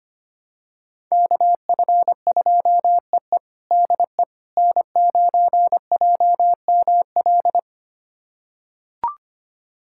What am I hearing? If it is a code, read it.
Morse code, heard as a keyed tone, about 25 wpm, 700 Hz. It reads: KF2EE DE N9JML
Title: USA Callsign DE Callsign - 5 Characters - 25wpm